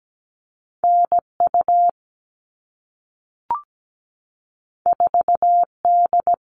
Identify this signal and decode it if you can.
Morse code, heard as a keyed tone, about 17 wpm, 700 Hz. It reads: NU 4D
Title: All Cut Numbers - 2-Digit 17wpm